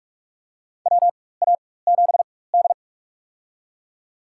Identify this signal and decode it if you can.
Morse code, heard as a keyed tone, about 47 wpm, spaced as for 21 wpm, 700 Hz. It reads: WA7D
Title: USA Call Signs - 4 Characters - 4x Character Spacing - 45wpm